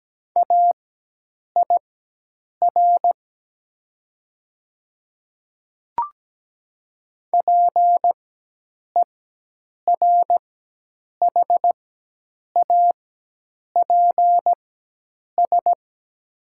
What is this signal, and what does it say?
Morse code, heard as a keyed tone, about 17 wpm, spaced as for 8 wpm, 700 Hz. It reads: AIR PERHAPS
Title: 4x Character Spacing - Top 500 Words - 17wpm